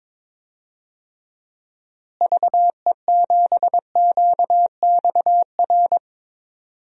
Morse code, heard as a keyed tone, about 21 wpm, 700 Hz. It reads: VE7QXR
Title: DXCC-100 Call Signs - 6 Characters - Repeated 2x - 22wpm